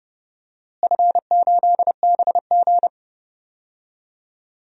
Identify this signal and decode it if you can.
Morse code, heard as a keyed tone, about 30 wpm, 700 Hz. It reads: F8BG
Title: DXCC-100 Call Signs - 4 Characters - Repeated 2x - 30wpm